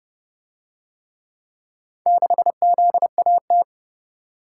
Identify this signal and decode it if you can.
Morse code, heard as a keyed tone, about 30 wpm, 700 Hz. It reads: 6ZAT